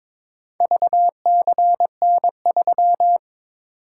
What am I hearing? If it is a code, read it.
Morse code, heard as a keyed tone, about 23 wpm, 700 Hz. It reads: VCN3
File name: practice / 22wpm / Four Letter-Number Combinations - 22wpm